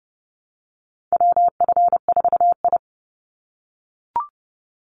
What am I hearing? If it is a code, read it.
Morse code, heard as a keyed tone, about 30 wpm, 700 Hz. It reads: WF4I
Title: NA SOTA Chaser Callsigns 30wpm